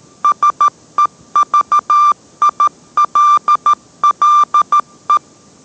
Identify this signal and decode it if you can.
Morse code, heard as a keyed tone, about 13 wpm, 1180 Hz. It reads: SEVILLE